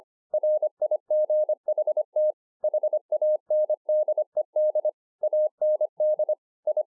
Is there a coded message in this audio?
Yes: Morse code from 0.3 to 6.8 s, heard as a keyed tone, about 25 wpm, 610 Hz. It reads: RIGHT HANDED AND I